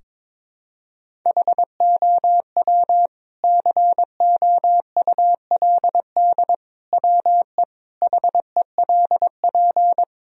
Morse code, heard as a keyed tone, about 22 wpm, 700 Hz. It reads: HOW COULD WE HELP
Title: Sentences from Top 1000 Words - Review Part 3 - 22wpm